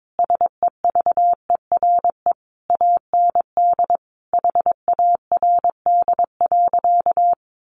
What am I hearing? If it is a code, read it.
Morse code, heard as a keyed tone, about 22 wpm, 700 Hz. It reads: SEVERE AND HARD.
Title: Sentences from Top 1400 Words 22wpm